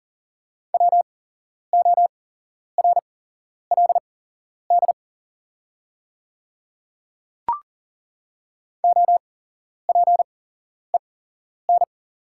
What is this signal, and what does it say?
Morse code, heard as a keyed tone, about 39 wpm, spaced as for 11 wpm, 690 Hz. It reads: WORLD OPEN
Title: 8x Character Spacing - Top 300 Words - 40wpm